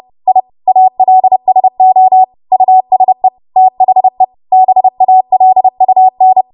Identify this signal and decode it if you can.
Morse code, heard as a keyed tone, about 30 wpm, 750 Hz. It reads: I ALSO USE THE BALUN
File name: Morse / 30WPM